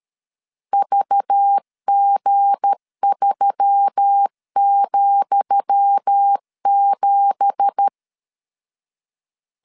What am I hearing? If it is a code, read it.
Morse code, heard as a keyed tone, about 13 wpm, 800 Hz. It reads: VG3,7
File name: LEZIONI / 10 SIGNS / 10D COMA 12wpm 2544